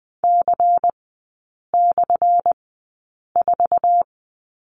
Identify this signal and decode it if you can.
Morse code, heard as a keyed tone, about 20 wpm, spaced as for 8 wpm, 700 Hz. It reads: C/4
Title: 191 Lesson 31e - Letter v - Three Characters 20wpm